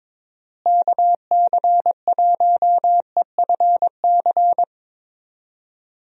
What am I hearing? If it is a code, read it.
Morse code, heard as a keyed tone, about 22 wpm, 700 Hz. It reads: KC1EFC
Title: USA Call Signs - 6 Characters - Repeated 3x - 22wpm